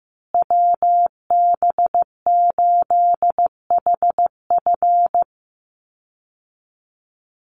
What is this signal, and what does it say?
Morse code, heard as a keyed tone, about 15 wpm, 700 Hz. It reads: WB8HF
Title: Call Signs Repeated 3x - 15wpm